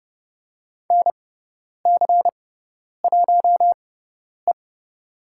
Morse code, heard as a keyed tone, about 30 wpm, spaced as for 10 wpm, 700 Hz. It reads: NC1E
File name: Lesson 16g - Letter c - Call Signs 30wpm